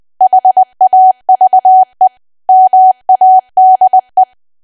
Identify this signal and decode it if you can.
Morse code, heard as a keyed tone, about 20 wpm, 750 Hz. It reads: HAVE MADE
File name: cwa / W1AW-Level3 / 20WPM